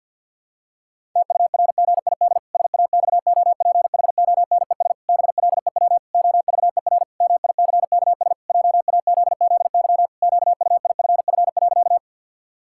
Morse code, heard as a keyed tone, about 52 wpm, 700 Hz. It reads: TURGID SAXOPHONES BLEW OVER MICKS JAZZY QUIFF.
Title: Pangram Challenge v2 - 50wpm